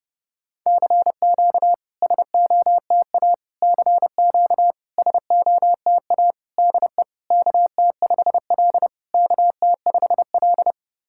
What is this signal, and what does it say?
Morse code, heard as a keyed tone, about 32 wpm, 690 Hz. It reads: CQ SOTA CQ SOTA DE KT5L KT5L
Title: SOTA QSO Practice 30wpm